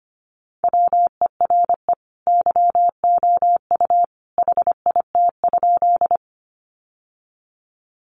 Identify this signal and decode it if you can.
Morse code, heard as a keyed tone, about 25 wpm, 700 Hz. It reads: WERE YOU HIT?
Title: Sentences from Top 1200 Words 25wpm